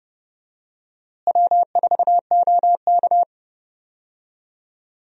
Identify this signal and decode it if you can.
Morse code, heard as a keyed tone, about 30 wpm, 700 Hz. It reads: W4OK
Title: USA Call Signs - 4 Characters - Repeated 3x - 30wpm